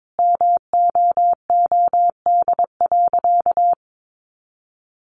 Morse code, heard as a keyed tone, about 22 wpm, 700 Hz. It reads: MOOD.
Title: Sentences from Top 1100 Words 22wpm